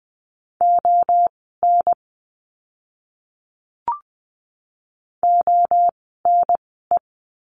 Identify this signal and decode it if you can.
Morse code, heard as a keyed tone, about 20 wpm, spaced as for 14 wpm, 700 Hz. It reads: ON ONE
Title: 036 Lesson 7f - Letter s - Words 20wpm